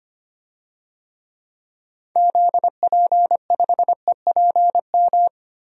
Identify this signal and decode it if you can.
Morse code, heard as a keyed tone, about 25 wpm, 700 Hz. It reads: ZP5EPM
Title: DXCC-100 Call Signs - 6 Characters - 25wpm